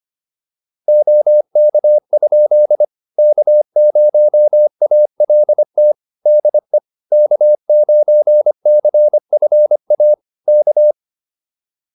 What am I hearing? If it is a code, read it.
Morse code, heard as a keyed tone, about 25 wpm, 600 Hz. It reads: OK? K0ALT DE K9CFA K
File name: QSO202 25